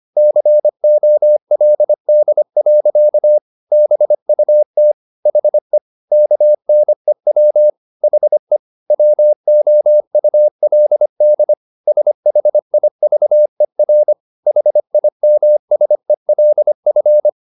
Morse code, heard as a keyed tone, about 25 wpm, 600 Hz. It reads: COLD. BUT HE KNEW HE WOULD SHIVER HIMSELF
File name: the old man and the sea part 1 25wpm